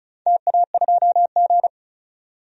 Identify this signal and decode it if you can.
Morse code, heard as a keyed tone, about 35 wpm, 700 Hz. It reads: TA2G
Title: DXCC-100 Call Signs - 4 Characters - 35wpm